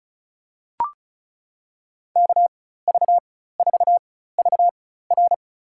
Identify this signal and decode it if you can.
Morse code, heard as a keyed tone, about 36 wpm, spaced as for 16 wpm, 700 Hz. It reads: KV4VR
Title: USA Call Signs - 5 Characters - 4x Character Spacing - 35wpm